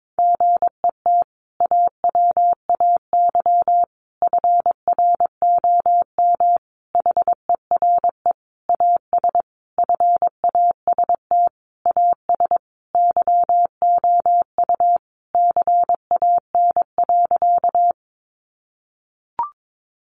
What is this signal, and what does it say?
Morse code, heard as a keyed tone, about 22 wpm, 700 Hz. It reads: GET AWAY FROM HERE AS FAST AS YOU CAN.
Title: Sentences from Top 600 Words 22wpm